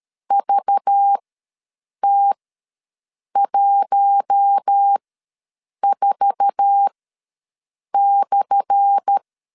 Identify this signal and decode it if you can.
Morse code, heard as a keyed tone, about 13 wpm, spaced as for 7 wpm, 800 Hz. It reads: VT14/
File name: LEZIONI / 10 SIGNS / 10D COMA ws 12wpm 3618